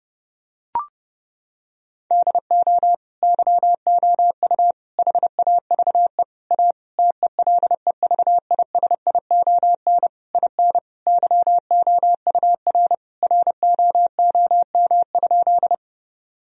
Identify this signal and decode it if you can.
Morse code, heard as a keyed tone, about 32 wpm, 700 Hz. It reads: DO YOU HAVE A TELEVISION IN YOUR ROOM?
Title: Sentences from Top 1000 Words - Review Part 4 - 30wpm